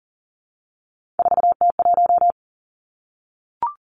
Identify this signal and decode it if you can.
Morse code, heard as a keyed tone, about 39 wpm, 700 Hz. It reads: 4T1
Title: Three Challenging Letter-Number Combinations 40wpm